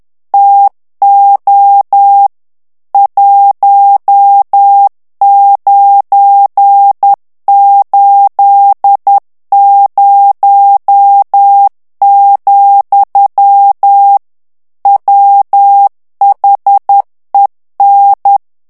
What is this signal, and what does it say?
Morse code, heard as a keyed tone, about 10 wpm, 800 Hz. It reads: TO 1980, WHEN
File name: ANGLAIS texte clair 3 vitesse 10 mots minute